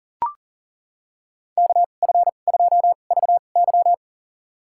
Morse code, heard as a keyed tone, about 40 wpm, spaced as for 29 wpm, 700 Hz. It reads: KF2VY
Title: USA Call Signs - 5 Characters - 2x Character Spacing - 40wpm